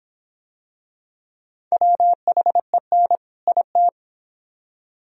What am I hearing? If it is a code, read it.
Morse code, heard as a keyed tone, about 25 wpm, 700 Hz. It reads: WHEN IT